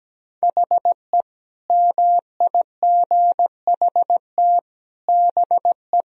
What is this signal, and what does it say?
Morse code, heard as a keyed tone, about 17 wpm, 700 Hz. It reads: HE MIGHT BE
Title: Sentences from Top 1000 Words - Review Part 2 - 17wpm